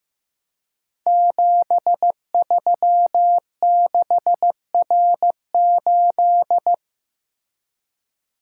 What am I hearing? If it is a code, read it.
Morse code, heard as a keyed tone, about 15 wpm, 700 Hz. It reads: 736R8
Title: Code Groups - 5 Letter:Numbers - 15wpm